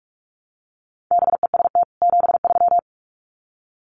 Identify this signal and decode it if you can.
Morse code, heard as a keyed tone, about 44 wpm, 700 Hz. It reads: BEST 73
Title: Common QSO Phrases 45wpm